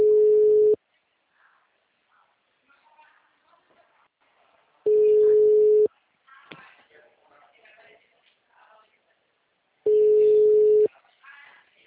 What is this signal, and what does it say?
Ringback tone (Europe, 425 Hz), with speech.